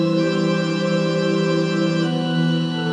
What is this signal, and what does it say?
There is no code.